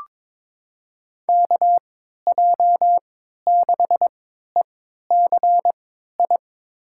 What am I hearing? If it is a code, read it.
Morse code, heard as a keyed tone, about 22 wpm, spaced as for 13 wpm, 700 Hz. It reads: KJ6ECI
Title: USA Call Signs - 6 Characters - 3x Character Spacing - 22wpm